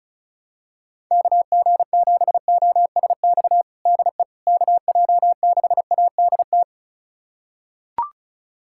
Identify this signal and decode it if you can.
Morse code, heard as a keyed tone, about 36 wpm, 700 Hz. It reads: KG7OSX DE KJ6ADT
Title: USA Callsign DE Callsign 35wpm